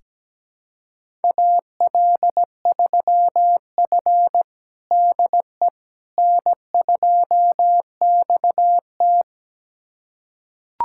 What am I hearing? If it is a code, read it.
Morse code, heard as a keyed tone, about 17 wpm, 700 Hz. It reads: AL3F DE N2XT
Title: USA Callsign DE Callsign - 4 Characters - 17wpm